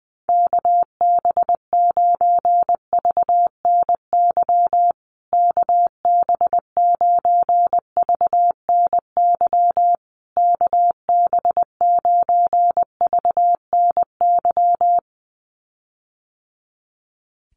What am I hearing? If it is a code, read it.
Morse code, heard as a keyed tone, about 20 wpm, 700 Hz. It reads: KB9VNY KB9VNY KB9VNY
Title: USA Call Signs Repeated 3x 20wpm